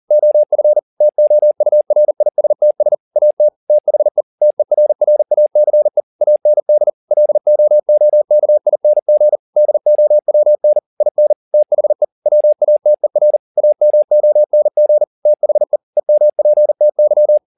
Morse code, heard as a keyed tone, about 40 wpm, 600 Hz. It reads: OF TOURISTS AT THE TERRACE AND LOOKING DOWN IN THE WATER AMONG THE EMPTY
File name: the old man and the sea 2 40wpm